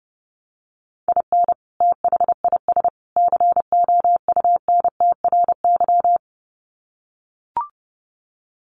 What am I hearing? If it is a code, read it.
Morse code, heard as a keyed tone, about 30 wpm, 700 Hz. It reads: IN THIS COUNTRY
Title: Sets of 3 Words 30wpm